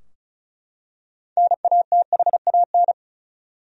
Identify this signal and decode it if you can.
Morse code, heard as a keyed tone, about 34 wpm, 700 Hz. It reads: NATHAN
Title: USA Common Names - 35wpm